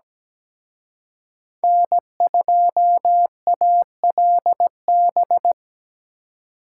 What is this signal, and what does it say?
Morse code, heard as a keyed tone, about 20 wpm, 700 Hz. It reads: N2ALB
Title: Speed Racing - USA 1x3 Call Signs 17wpm